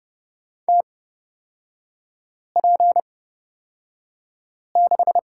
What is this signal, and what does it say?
Morse code, heard as a keyed tone, about 30 wpm, spaced as for 5 wpm, 700 Hz. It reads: TP6
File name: Lesson 36e - BT - Three Characters 30wpm